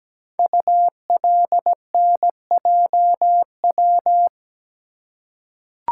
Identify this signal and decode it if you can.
Morse code, heard as a keyed tone, about 17 wpm, 700 Hz. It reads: ULNJW
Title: Code Groups - 5 Letters - 17wpm